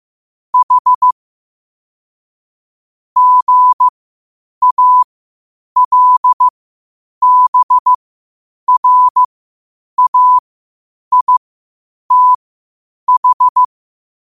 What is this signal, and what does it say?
Morse code, heard as a keyed tone, about 15 wpm, spaced as for 8 wpm, 1000 Hz. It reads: H GALBRAITH
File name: qotdfiles / 8wpm